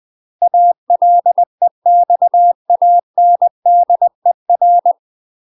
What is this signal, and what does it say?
Morse code, heard as a keyed tone, about 20 wpm, 700 Hz. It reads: ALEXANDER
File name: UK MASTER QSOs TRAINING / 20WPM / UK MASTER QSOs 0001